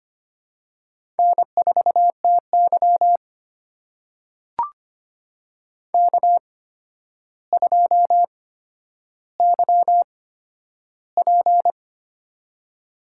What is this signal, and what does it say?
Morse code, heard as a keyed tone, about 25 wpm, spaced as for 11 wpm, 700 Hz. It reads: N4TY K2YP
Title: USA Call Signs - 4 Characters - 8x Character Spacing - 25wpm